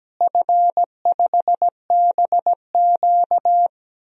Morse code, heard as a keyed tone, about 17 wpm, 700 Hz. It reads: F5BQ
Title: DXCC-100 Call Signs - 4 Characters - Repeated 2x - 17wpm